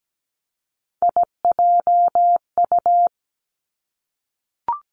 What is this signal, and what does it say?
Morse code, heard as a keyed tone, about 17 wpm, 700 Hz. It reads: IJU
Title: Three Challenging Letter-Number Combinations 17wpm